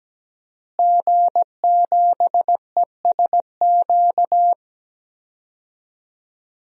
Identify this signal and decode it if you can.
Morse code, heard as a keyed tone, about 17 wpm, 700 Hz. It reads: G7ESQ
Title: DXCC-100 Call Signs - 5 Characters - Repeated 3x - 17wpm